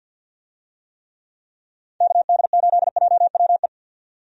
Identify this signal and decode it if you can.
Morse code, heard as a keyed tone, about 52 wpm, 700 Hz. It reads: KD8JWE
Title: USA Call Signs - 6 Characters - 50wpm